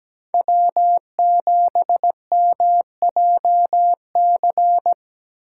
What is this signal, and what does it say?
Morse code, heard as a keyed tone, about 17 wpm, 700 Hz. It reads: W7MJC
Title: USA Call Signs - 5 Characters - 8x Character Spacing - 17wpm